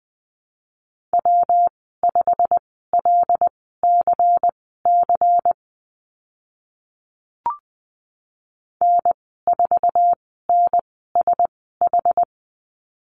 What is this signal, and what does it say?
Morse code, heard as a keyed tone, about 20 wpm, spaced as for 15 wpm, 700 Hz. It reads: W5LCC N4NSH
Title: 154 Lesson 25g - Letter p - Call Signs 20wpm